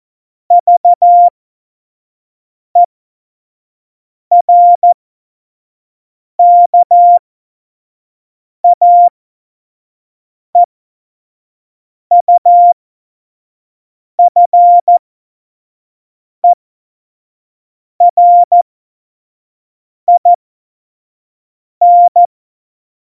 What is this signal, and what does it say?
Morse code, heard as a keyed tone, about 14 wpm, spaced as for 5 wpm, 700 Hz. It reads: VERKAEUFERIN
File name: Morsekurs / Witze / Witz 10 April 2018, 25 F70